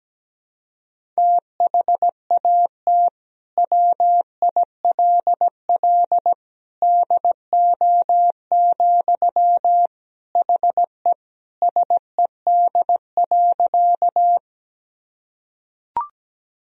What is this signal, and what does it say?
Morse code, heard as a keyed tone, about 17 wpm, 700 Hz. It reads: THAT WILL DO, HE SED.
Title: Sentences from Top 100 Words and Abbreviations 17wpm